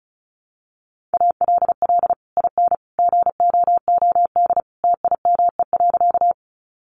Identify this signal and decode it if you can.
Morse code, heard as a keyed tone, about 34 wpm, 700 Hz. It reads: ALL IN GOOD TIME.
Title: Sentences from Top 200 Words 35wpm